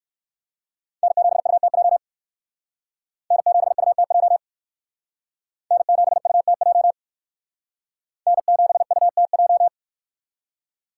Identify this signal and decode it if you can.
Morse code, heard as a keyed tone, about 56 wpm, 700 Hz. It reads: N7UTJ N7UTJ N7UTJ N7UTJ
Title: Speed Racing - USA 1x3 Call Signs 45wpm